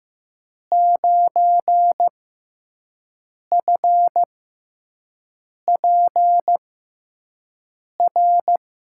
Morse code, heard as a keyed tone, about 15 wpm, spaced as for 5 wpm, 700 Hz. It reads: 9FPR
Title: USA Call Signs - 6 Characters - 6x Character Spacing - 15wpm